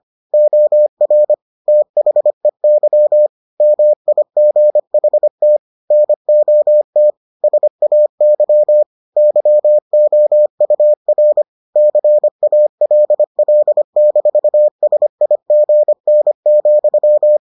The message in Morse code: OR THEY MIGHT NOT SAY YOUR CALL-SIGN,

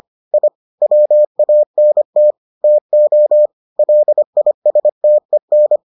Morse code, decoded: I WANT TO LISTEN

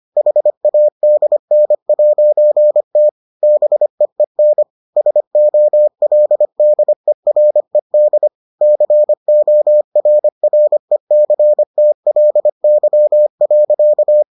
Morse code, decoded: HADN'T BEEN SOLDERED CORRECTLY.

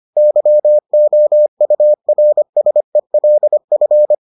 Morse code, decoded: YOURSELF

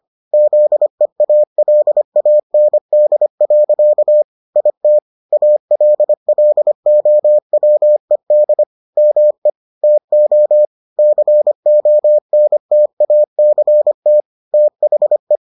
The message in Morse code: ZEALAND. IT ALLOWED ME TO CONTACT THE

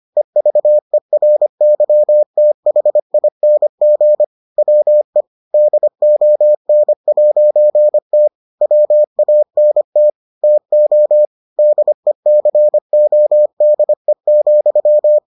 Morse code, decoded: EVERYTHING WE DON'T WANT TO DECODE,

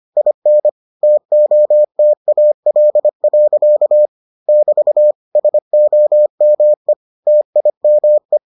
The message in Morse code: IN TOTAL. <BT> SOME TIME

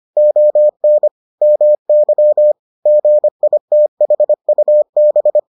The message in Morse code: ON MY GITHUB

